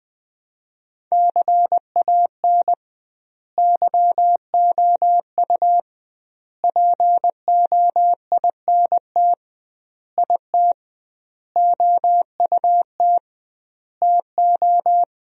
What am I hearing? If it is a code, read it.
Morse code, heard as a keyed tone, about 20 wpm, 700 Hz. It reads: CAN YOU POINT IT OUT TO
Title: 2x Word-Spacing - Sentences from Top 200 Words - 20wpm